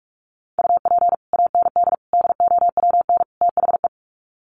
Morse code, heard as a keyed tone, about 44 wpm, 700 Hz. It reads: UP AND DOWN THE
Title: Sets of 4 Words 45wpm